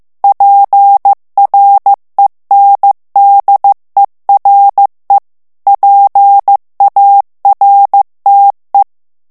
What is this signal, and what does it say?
Morse code, heard as a keyed tone, about 15 wpm, 800 Hz. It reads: PRENDERE PARTE
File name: morse mp3 / ITALIEN texte clair 2 vitesse 14 mots minute